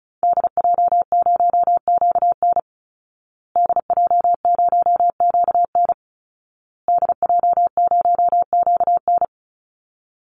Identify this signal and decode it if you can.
Morse code, heard as a keyed tone, about 34 wpm, 700 Hz. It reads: DJ0QN DJ0QN DJ0QN
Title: Call Signs Repeated 2x - 35wpm